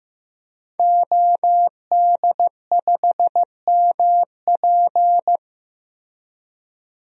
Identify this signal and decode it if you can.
Morse code, heard as a keyed tone, about 15 wpm, 700 Hz. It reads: OD5MP
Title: DXCC-100 Call Signs - 5 Characters - Repeated 2x - 15wpm